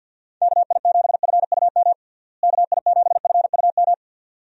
Morse code, heard as a keyed tone, about 50 wpm, 700 Hz. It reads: KI7FUM KI7FUM
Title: USA Call Signs - 6 Characters - Repeated 2x - 50wpm